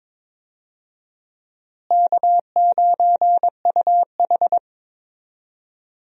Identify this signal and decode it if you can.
Morse code, heard as a keyed tone, about 22 wpm, 700 Hz. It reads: K9UH
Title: USA Call Signs - 4 Characters - Repeated 2x - 22wpm